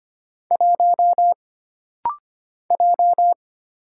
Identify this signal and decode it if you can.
Morse code, heard as a keyed tone, about 25 wpm, 700 Hz. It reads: 1 J
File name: Single Character (Letter:Number:Punctuation:Prosign) - Rapid-Fire - 25wpm